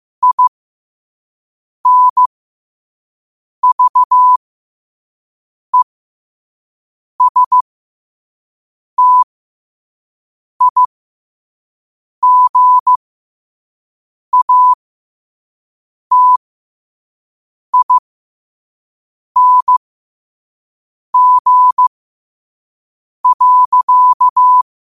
Morse code, heard as a keyed tone, about 15 wpm, spaced as for 5 wpm, 1000 Hz. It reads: INVESTIGATING.
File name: qotdfiles / 5wpm